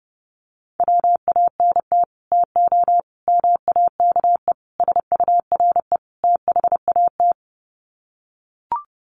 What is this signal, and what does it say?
Morse code, heard as a keyed tone, about 30 wpm, 700 Hz. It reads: WANT TO MAKE SURE THAT
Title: Sets of 5 Words 30wpm